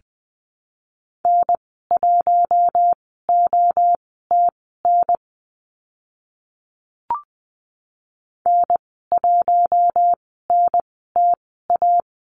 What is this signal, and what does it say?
Morse code, heard as a keyed tone, about 20 wpm, spaced as for 14 wpm, 700 Hz. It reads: N1OTN N1NTA
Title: 042 Lesson 8g - Number 1 - Call Signs 20wpm